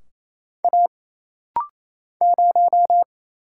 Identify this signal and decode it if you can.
Morse code, heard as a keyed tone, about 27 wpm, 700 Hz. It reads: A 0
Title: Single Letter-Number Alt - Rapid-Fire 28wpm